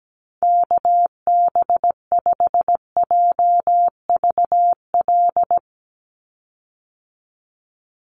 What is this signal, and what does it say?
Morse code, heard as a keyed tone, about 17 wpm, 700 Hz. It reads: KB5JVL
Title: USA Call Signs 17wpm